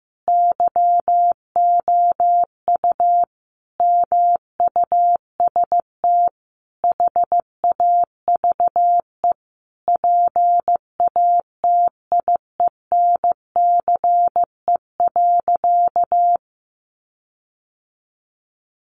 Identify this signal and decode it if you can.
Morse code, heard as a keyed tone, about 14 wpm, 700 Hz. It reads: YOU MUST HAVE PATIENCE.
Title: Sentences from Top 1300 Words 15wpm